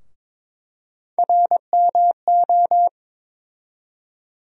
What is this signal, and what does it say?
Morse code, heard as a keyed tone, about 22 wpm, 700 Hz. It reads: RMO
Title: USA Call Sign Postfixes - 3 Letters 22wpm